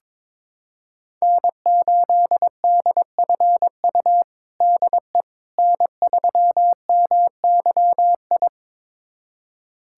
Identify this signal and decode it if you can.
Morse code, heard as a keyed tone, about 22 wpm, 700 Hz. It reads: N8DFU DE N3MYI
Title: USA Callsign DE Callsign - 5 Characters - 22wpm